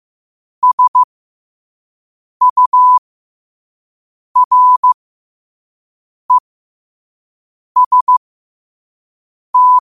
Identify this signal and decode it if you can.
Morse code, heard as a keyed tone, about 15 wpm, spaced as for 5 wpm, 1000 Hz. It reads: SUREST
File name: qotdfiles / 5wpm